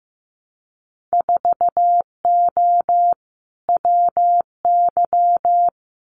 Morse code, heard as a keyed tone, about 15 wpm, 700 Hz. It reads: 4O WY
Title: Winter Field Day Exchanges Encore 15wpm